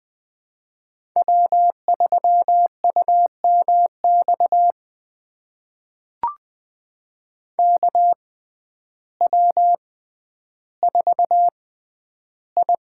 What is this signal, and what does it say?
Morse code, heard as a keyed tone, about 20 wpm, spaced as for 11 wpm, 700 Hz. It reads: W3UMX KW4I
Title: USA Call Signs - 5 Characters - 6x Character Spacing - 20wpm